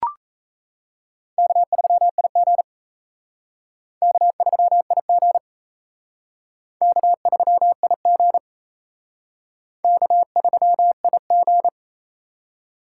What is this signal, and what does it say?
Morse code, heard as a keyed tone, about 35 wpm, 700 Hz. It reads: K3IG K3IG K3IG K3IG
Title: Speed Racing - USA 1x2 Call Signs 28wpm